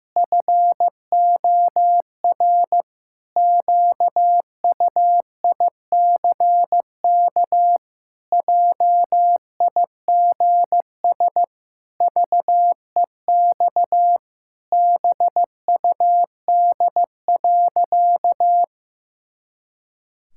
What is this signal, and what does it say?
Morse code, heard as a keyed tone, about 15 wpm, 700 Hz. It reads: FOR QUICK JIGS VEX BUD.
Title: Pangram Challenge v2 - 15wpm